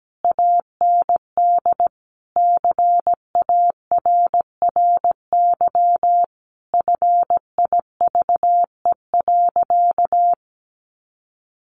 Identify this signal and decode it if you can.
Morse code, heard as a keyed tone, about 17 wpm, 700 Hz. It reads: AND CARRY FIVE.
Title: Sentences from Top 400 Words 17wpm